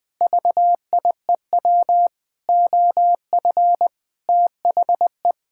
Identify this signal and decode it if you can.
Morse code, heard as a keyed tone, about 20 wpm, 700 Hz. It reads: VIEW OF THE
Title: Sentences with 7-Word Phrases - 20wpm